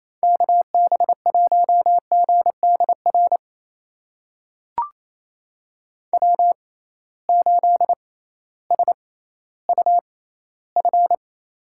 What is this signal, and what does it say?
Morse code, heard as a keyed tone, about 28 wpm, spaced as for 15 wpm, 700 Hz. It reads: KB1GDR W8SUF
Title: USA Call Signs - 6x Character Spacing - 28wpm